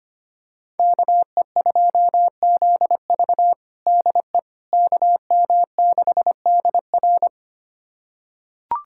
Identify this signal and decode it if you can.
Morse code, heard as a keyed tone, about 25 wpm, 700 Hz. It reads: KE2ZV DE KM6DR
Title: USA Callsign DE Callsign - 5 Characters - 25wpm